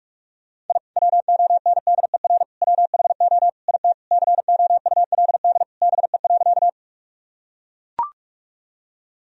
Morse code, heard as a keyed tone, about 41 wpm, 700 Hz. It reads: I WONDER WHO IT COULD BE.
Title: Sentences from Top 1000 Words - Review Part 3 - 45wpm